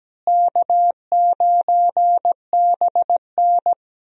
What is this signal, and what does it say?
Morse code, heard as a keyed tone, about 17 wpm, 700 Hz. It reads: K9BN
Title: USA Call Signs - 4 Characters - Repeated 2x - 17wpm